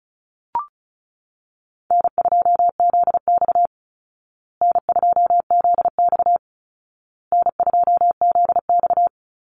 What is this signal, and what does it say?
Morse code, heard as a keyed tone, about 34 wpm, 700 Hz. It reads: N2ZX N2ZX N2ZX
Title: Call Signs Repeated 2x - 35wpm